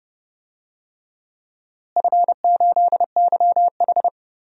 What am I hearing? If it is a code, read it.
Morse code, heard as a keyed tone, about 30 wpm, 700 Hz. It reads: F8YH
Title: DXCC-100 Call Signs - 4 Characters - Repeated 3x - 30wpm